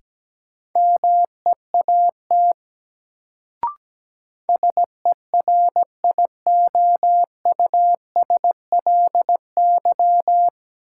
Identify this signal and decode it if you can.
Morse code, heard as a keyed tone, about 17 wpm, 700 Hz. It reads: MEAT SERIOUSLY
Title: Sentences from Top 1200 Words - New Word Drill Alt - 17wpm